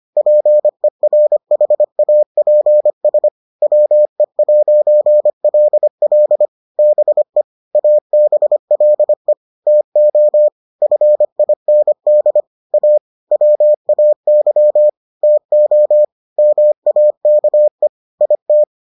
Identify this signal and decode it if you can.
Morse code, heard as a keyed tone, about 25 wpm, 600 Hz. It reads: PERHAPS WE'LL BE ABLE TO FIND A WAY TO MAKE IT